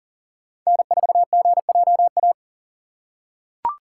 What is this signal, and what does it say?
Morse code, heard as a keyed tone, about 40 wpm, 700 Hz. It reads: N4GJA
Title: USA Call Signs - 5 Characters - 6x Character Spacing - 40wpm